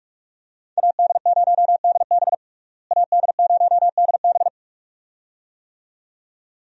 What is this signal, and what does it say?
Morse code, heard as a keyed tone, about 42 wpm, 700 Hz. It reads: AD0DB AD0DB
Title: USA Call Signs - 5 Characters - Repeated 2x - 45wpm